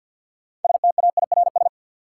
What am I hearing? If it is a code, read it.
Morse code, heard as a keyed tone, about 49 wpm, 700 Hz. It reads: STAIRS
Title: Sets of 5 Words Encore - New Word Drill - 50wpm